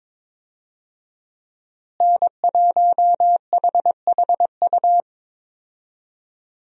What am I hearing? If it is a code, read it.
Morse code, heard as a keyed tone, about 22 wpm, 700 Hz. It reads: N1HHU
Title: USA Call Signs - 5 Characters - Repeated 2x - 22wpm